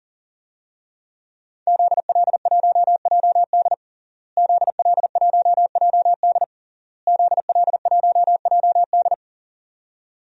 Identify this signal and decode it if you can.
Morse code, heard as a keyed tone, about 40 wpm, 700 Hz. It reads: ZL1JD ZL1JD ZL1JD
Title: DXCC-100 Call Signs - 5 Characters - Repeated 3x - 40wpm